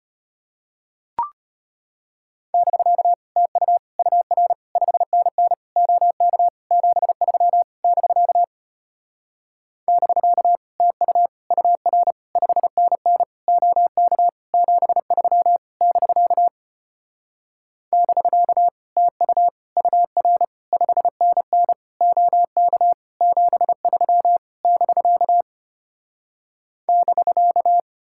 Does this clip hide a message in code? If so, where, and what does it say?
Yes: Morse code from 2.5 to 27.8 s, heard as a keyed tone, about 33 wpm, 700 Hz. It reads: <BK> TU UR 5NN OK 73 <BK> <BK> TU UR 5NN OK 73 <BK> <BK> TU UR 5NN OK 73 <BK> <BK>